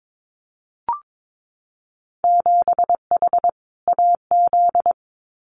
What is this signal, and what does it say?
Morse code, heard as a keyed tone, about 22 wpm, 700 Hz. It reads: 7H AZ
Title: Winter Field Day Exchanges Encore 22wpm